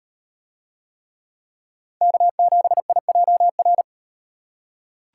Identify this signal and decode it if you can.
Morse code, heard as a keyed tone, about 35 wpm, 700 Hz. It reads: K7IJR